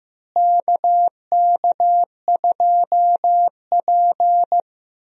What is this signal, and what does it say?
Morse code, heard as a keyed tone, about 15 wpm, 700 Hz. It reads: KK2P
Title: USA Call Signs - 4 Characters - Repeated 3x - 15wpm